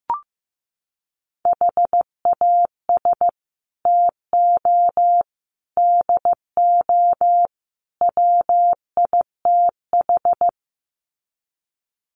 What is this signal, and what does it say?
Morse code, heard as a keyed tone, about 15 wpm, 700 Hz. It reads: HAS TO DO WITH
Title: Sets of 4 Words 15wpm